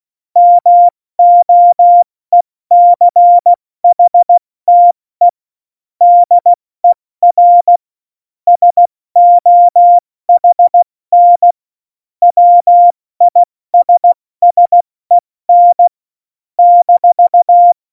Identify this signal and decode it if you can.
Morse code, heard as a keyed tone, about 16 wpm, spaced as for 14 wpm, 700 Hz. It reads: MOECHTE DER SOHN WISSEN -
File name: Morsekurs / Witze / Witz 11. Juni 2019, 70 F80